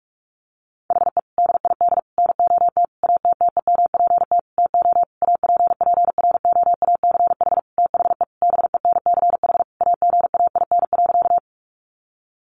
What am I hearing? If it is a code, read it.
Morse code, heard as a keyed tone, about 44 wpm, 700 Hz. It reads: HE DID NOT ATTEMPT TO APPROACH THE BENCH AGAIN.
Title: Sentences from Top 1400 Words 45wpm